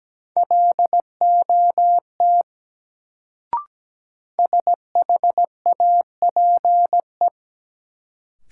Morse code, heard as a keyed tone, about 17 wpm, 700 Hz. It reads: LOT SHAPE
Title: Sentences from Top 500 Words - New Word Drill - 17wpm